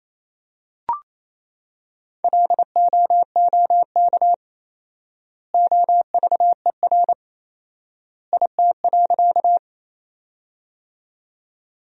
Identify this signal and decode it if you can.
Morse code, heard as a keyed tone, about 27 wpm, 700 Hz. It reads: LOOK OVER IT.
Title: 4x Word-Spacing - Sentences from Top 100 Words - 28wpm